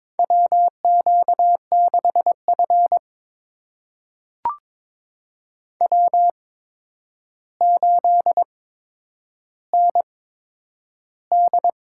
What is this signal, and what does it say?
Morse code, heard as a keyed tone, about 22 wpm, spaced as for 9 wpm, 700 Hz. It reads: WQ6F W8ND
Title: USA Call Signs - 8x Character Spacing - 22wpm